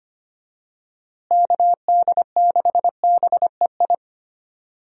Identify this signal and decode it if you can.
Morse code, heard as a keyed tone, about 25 wpm, 700 Hz. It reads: KD6BEI
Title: USA Call Signs - 6 Characters - Repeated 3x - 25wpm